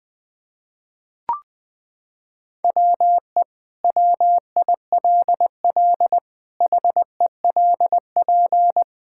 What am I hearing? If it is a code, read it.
Morse code, heard as a keyed tone, about 21 wpm, 700 Hz. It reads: WE WILL HELP
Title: Sentences with 2-Word Phrases - 20wpm